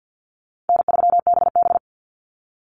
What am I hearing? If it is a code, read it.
Morse code, heard as a keyed tone, about 50 wpm, 700 Hz. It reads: N3BB
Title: Easy Call Signs - 50wpm